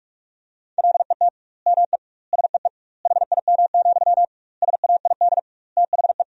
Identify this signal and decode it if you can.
Morse code, heard as a keyed tone, about 41 wpm, 700 Hz. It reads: LET ME SEE HIM, SAID THE
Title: 2x Word-Spacing - Sentences from Top 400 Words - 45wpm